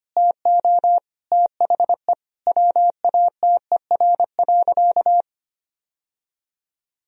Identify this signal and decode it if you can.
Morse code, heard as a keyed tone, about 25 wpm, 700 Hz. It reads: TO THE WATER.
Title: Sentences from Top 1000 Words - Review Part 4 - 25wpm